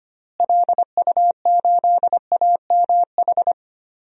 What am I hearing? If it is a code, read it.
Morse code, heard as a keyed tone, about 25 wpm, 700 Hz. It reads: LU8AMH
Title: DXCC-100 Call Signs - 6 Characters - Repeated 2x - 25wpm